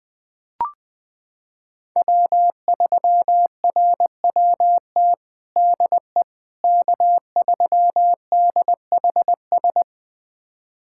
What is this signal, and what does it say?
Morse code, heard as a keyed tone, about 20 wpm, 700 Hz. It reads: W3RWT DE K3DHS
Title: USA Callsign DE Callsign - 5 Characters - 20wpm